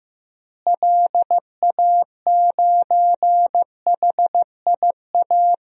Morse code, heard as a keyed tone, about 16 wpm, 700 Hz. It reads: LA9HIA
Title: DXCC-100 Call Signs - 6 Characters - Repeated 3x - 15wpm